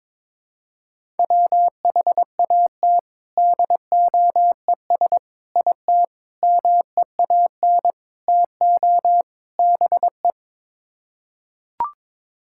Morse code, heard as a keyed tone, about 22 wpm, 700 Hz. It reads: WHAT DOES IT MEAN TO BE
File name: practice / 22wpm / Sets of 6 Words - 22wpm